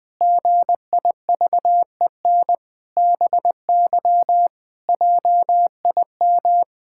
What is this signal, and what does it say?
Morse code, heard as a keyed tone, about 21 wpm, 700 Hz. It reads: GIVEN BY JIM
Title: Pangram Challenge v2 - 20wpm